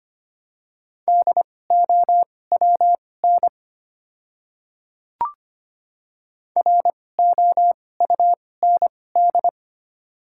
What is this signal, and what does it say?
Morse code, heard as a keyed tone, about 25 wpm, spaced as for 18 wpm, 700 Hz. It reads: DOWN ROUND
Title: 2x Character Spacing - Top 200 Words - 25wpm